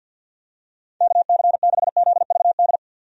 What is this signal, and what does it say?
Morse code, heard as a keyed tone, about 47 wpm, 700 Hz. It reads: KC6ZVD
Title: USA Call Signs - 6 Characters - 50wpm